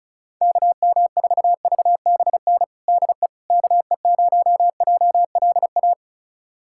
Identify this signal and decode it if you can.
Morse code, heard as a keyed tone, about 36 wpm, 700 Hz. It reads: KM4VBN DE KE0JLA
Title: USA Callsign DE Callsign - 6 Characters - 35wpm